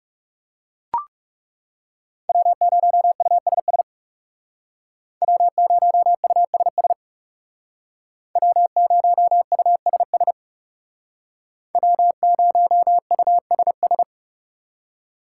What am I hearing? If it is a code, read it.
Morse code, heard as a keyed tone, about 38 wpm, 700 Hz. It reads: W0USS W0USS W0USS W0USS